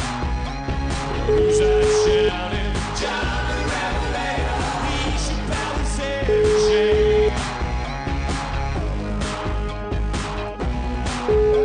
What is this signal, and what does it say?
Ringback tone (Europe, 425 Hz), with noise.